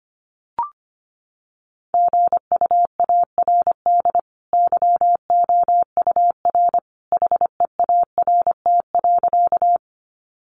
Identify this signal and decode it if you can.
Morse code, heard as a keyed tone, about 25 wpm, 700 Hz. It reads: GUARD YOUR HEART.
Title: Sentences from Top 1200 Words 25wpm